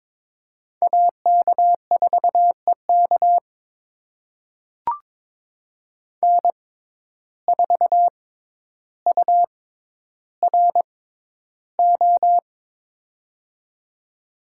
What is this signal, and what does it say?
Morse code, heard as a keyed tone, about 22 wpm, spaced as for 11 wpm, 700 Hz. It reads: AK4EK N4URO
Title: USA Call Signs - 5 Characters - 6x Character Spacing - 22wpm